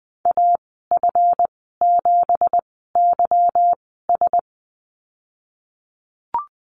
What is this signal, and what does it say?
Morse code, heard as a keyed tone, about 20 wpm, spaced as for 15 wpm, 700 Hz. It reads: AF7YS
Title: 173 Lesson 28g - Number 9 - Call Signs 20wpm